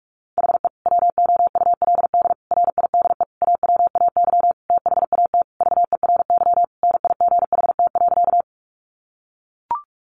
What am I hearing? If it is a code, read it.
Morse code, heard as a keyed tone, about 44 wpm, 700 Hz. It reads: HE WOULD RIDE AWAY THAT VERY NIGHT.
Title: Sentences from Top 1200 Words 45wpm